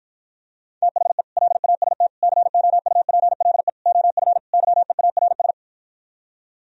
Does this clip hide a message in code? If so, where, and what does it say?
Yes: Morse code from 0.8 to 5.5 s, heard as a keyed tone, about 55 wpm, 700 Hz. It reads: THE LAST COUPLE OF YEARS